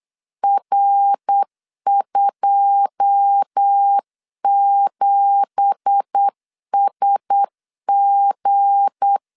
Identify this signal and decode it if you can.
Morse code, heard as a keyed tone, about 8 wpm, 800 Hz. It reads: R27SG
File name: CW / DET / 1 9 1500C S A 8wpm 5637